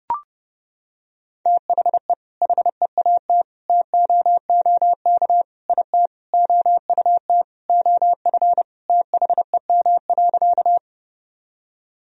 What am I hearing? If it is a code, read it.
Morse code, heard as a keyed tone, about 32 wpm, 700 Hz. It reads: THE HEAT TOOK IT OUT OF THEM.
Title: Sentences from Top 1000 Words - Review Part 3 - 30wpm